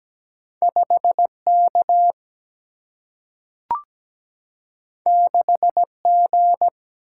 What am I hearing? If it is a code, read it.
Morse code, heard as a keyed tone, about 17 wpm, 700 Hz. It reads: 5K 6G
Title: Single-Letter Single-Number 17wpm